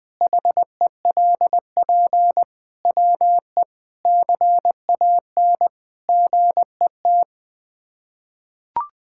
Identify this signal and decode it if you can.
Morse code, heard as a keyed tone, about 21 wpm, 700 Hz. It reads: HELP WE CAN GET
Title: Sets of 8 Words - 20wpm